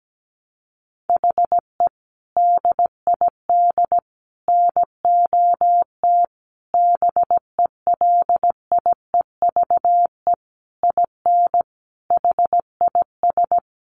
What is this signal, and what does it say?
Morse code, heard as a keyed tone, about 17 wpm, 700 Hz. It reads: HE DID NOT BELIEVE IN HIS
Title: Sentences from Top 1300 Words 17wpm